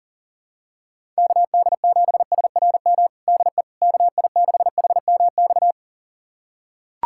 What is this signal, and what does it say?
Morse code, heard as a keyed tone, about 40 wpm, 700 Hz. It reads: KD7SRM DE KI6HMX
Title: USA Callsign DE Callsign 40wpm